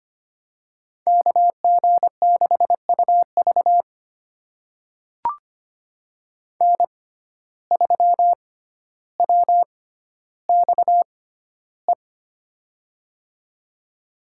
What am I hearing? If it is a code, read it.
Morse code, heard as a keyed tone, about 25 wpm, spaced as for 13 wpm, 700 Hz. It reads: KG6UV N3WXE
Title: USA Call Signs - 5 Characters - 6x Character Spacing - 25wpm